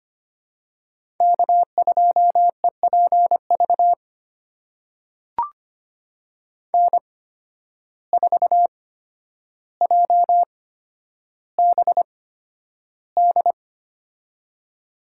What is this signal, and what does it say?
Morse code, heard as a keyed tone, about 25 wpm, spaced as for 11 wpm, 700 Hz. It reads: K2EPV N4JBD
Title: USA Call Signs - 5 Characters - 8x Character Spacing - 25wpm